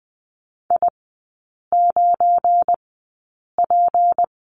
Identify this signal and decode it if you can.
Morse code, heard as a keyed tone, about 20 wpm, spaced as for 8 wpm, 700 Hz. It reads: I9P
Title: 172 Lesson 28e - Number 9 - Three Characters 20wpm